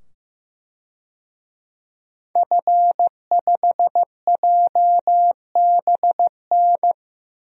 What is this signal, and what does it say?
Morse code, heard as a keyed tone, about 15 wpm, 700 Hz. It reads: F5JBN
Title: DXCC-100 Call Signs - 5 Characters - 15wpm